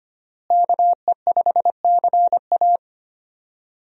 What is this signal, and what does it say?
Morse code, heard as a keyed tone, about 25 wpm, 690 Hz. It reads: KE5CA
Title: USA Call Signs - 5 Characters - Repeated 2x - 25wpm